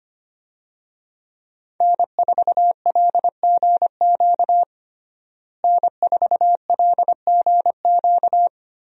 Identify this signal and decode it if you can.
Morse code, heard as a keyed tone, about 25 wpm, 700 Hz. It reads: N4LGQ N4LGQ
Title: USA Call Signs - 5 Characters - Repeated 2x - 25wpm